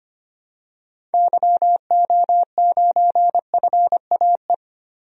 Morse code, heard as a keyed tone, about 25 wpm, 700 Hz. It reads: YO9FAE
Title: DXCC-100 Call Signs - 6 Characters - 25wpm